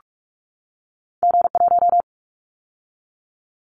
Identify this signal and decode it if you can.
Morse code, heard as a keyed tone, about 46 wpm, 700 Hz. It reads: G1
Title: Callsign Prefixes for Top 25 DXCC - 45wpm